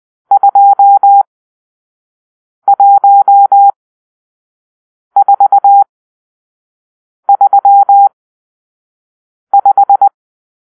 Morse code, heard as a keyed tone, about 20 wpm, spaced as for 5 wpm, 800 Hz. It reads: 21435
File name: arrl / Grp-6 Drill-2